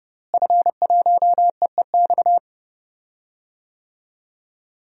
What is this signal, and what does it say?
Morse code, heard as a keyed tone, about 32 wpm, 700 Hz. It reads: F1EEX
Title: DXCC-100 Call Signs - 5 Characters - Repeated 2x - 30wpm